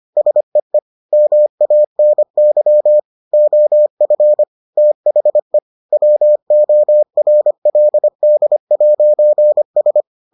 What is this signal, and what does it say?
Morse code, heard as a keyed tone, about 25 wpm, 600 Hz. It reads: SEE MANY OF THE WORLD'S